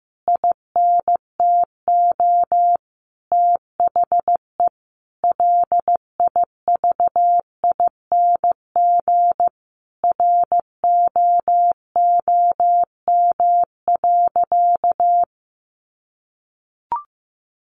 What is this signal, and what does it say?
Morse code, heard as a keyed tone, about 15 wpm, 700 Hz. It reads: INTO THE LIVING ROOM.
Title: Sentences from Top 1500 Words 15wpm